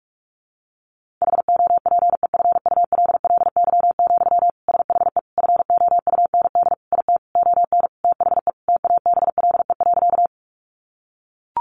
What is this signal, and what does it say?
Morse code, heard as a keyed tone, about 44 wpm, 700 Hz. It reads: HOPEFULLY, SHE FOUND IT ON THE TABLE.
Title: Sentences from Top 1500 Words 45wpm